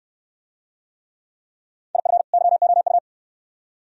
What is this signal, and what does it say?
Morse code, heard as a keyed tone, about 65 wpm, 690 Hz. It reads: IF YOU